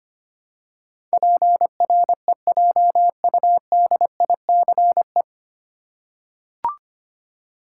Morse code, heard as a keyed tone, about 25 wpm, 700 Hz. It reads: PREJUDICE
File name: practice / 25wpm / 9 Letter Words - First 3 Letters Given - 25wpm